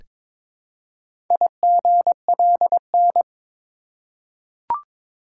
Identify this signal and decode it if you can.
Morse code, heard as a keyed tone, about 22 wpm, 700 Hz. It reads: IGLN
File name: practice / 22wpm / Four Letter-Number Combinations - 22wpm